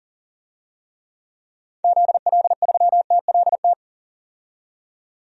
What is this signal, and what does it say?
Morse code, heard as a keyed tone, about 37 wpm, 700 Hz. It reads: ZL3TLT
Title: DXCC-100 Call Signs - 6 Characters - 40wpm